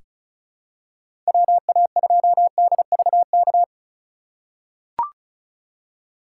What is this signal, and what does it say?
Morse code, heard as a keyed tone, about 36 wpm, 700 Hz. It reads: WA2DVK
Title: USA Call Signs - 6 Characters - 3x Character Spacing - 35wpm